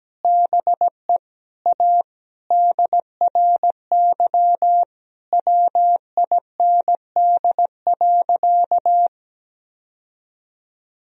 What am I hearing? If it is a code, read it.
Morse code, heard as a keyed tone, about 17 wpm, 700 Hz. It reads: BE A DRY WIND.
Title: Sentences from Top 1000 Words - Review Part 3 - 17wpm